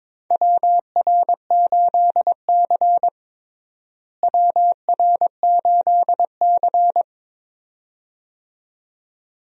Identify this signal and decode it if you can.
Morse code, heard as a keyed tone, about 22 wpm, 700 Hz. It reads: WR8C WR8C
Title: USA Call Signs - 4 Characters - Repeated 2x - 22wpm